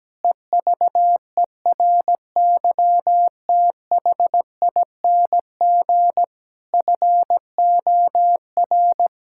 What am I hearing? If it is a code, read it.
Morse code, heard as a keyed tone, about 17 wpm, 700 Hz. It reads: EVERYTHING FOR
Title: Sentences from Top 1000 Words - Review Part 6 - 17wpm